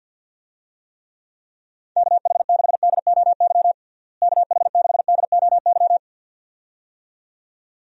Morse code, heard as a keyed tone, about 50 wpm, 700 Hz. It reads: KH6DOY KH6DOY
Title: DXCC-100 Call Signs - 6 Characters - Repeated 2x - 50wpm